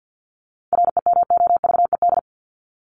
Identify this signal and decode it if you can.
Morse code, heard as a keyed tone, about 50 wpm, 700 Hz. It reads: REMOVED
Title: ED Suffix - Top 100 Words with Base Words 50wpm